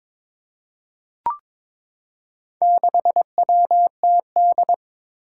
Morse code, heard as a keyed tone, about 22 wpm, 700 Hz. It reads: 6WTD
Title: Four Challenging Letter-Number Combinations 22wpm